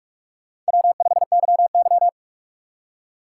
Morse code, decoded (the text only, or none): W5YY